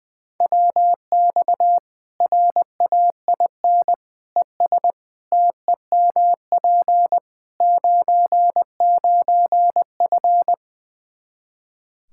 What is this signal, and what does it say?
Morse code, heard as a keyed tone, about 20 wpm, 700 Hz. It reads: WX RAIN ES TEMP 99F
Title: Practice QSO - Exchange 2 - Info - 20wpm